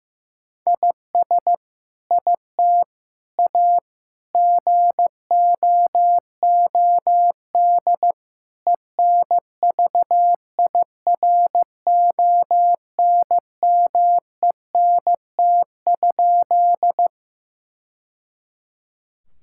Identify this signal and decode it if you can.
Morse code, heard as a keyed tone, about 16 wpm, 700 Hz. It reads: IS IT A GOOD ENVIRONMENT?
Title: Sentences from Top 1000 Words - Review Part 5 - 15wpm